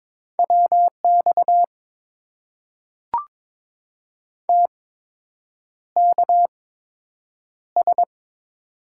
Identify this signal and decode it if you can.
Morse code, heard as a keyed tone, about 22 wpm, spaced as for 8 wpm, 700 Hz. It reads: WX TKS
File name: Top 90 QSO Elements - 8x Character Spacing - 22wpm